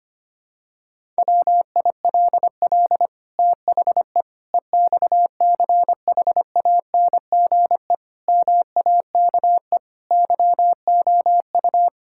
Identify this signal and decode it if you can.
Morse code, heard as a keyed tone, about 25 wpm, 700 Hz. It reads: WILL THE EXCHANGE MAKE YOU
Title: Sentences from Top 1000 Words - Review Part 6 - 25wpm